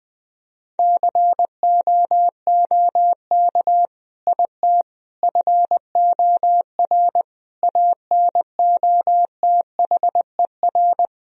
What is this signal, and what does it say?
Morse code, heard as a keyed tone, about 20 wpm, 700 Hz. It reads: COOK IT FOR ANOTHER
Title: Sentences from Top 1000 Words - Review Part 4 - 20wpm